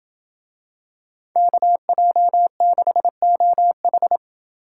Morse code, heard as a keyed tone, about 27 wpm, 700 Hz. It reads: KJ6OH